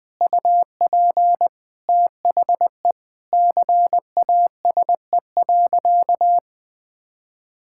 Morse code, heard as a keyed tone, about 20 wpm, 700 Hz. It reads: UP THE CASE.
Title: All-Age Appropriate Jokes - 20wpm